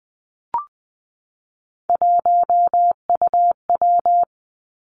Morse code, heard as a keyed tone, about 20 wpm, 700 Hz. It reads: 1UW
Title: Three Challenging Letter-Number Combinations 20wpm